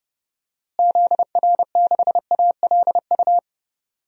Morse code, heard as a keyed tone, about 30 wpm, 700 Hz. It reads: ZR6ALU
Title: DXCC-100 Call Signs - 6 Characters - Repeated 3x - 30wpm